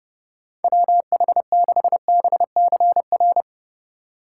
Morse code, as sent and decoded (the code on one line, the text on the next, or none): .-- .... -.... -... -.-. .-.
WH6BCR